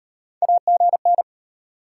Morse code, heard as a keyed tone, about 34 wpm, 700 Hz. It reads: AGN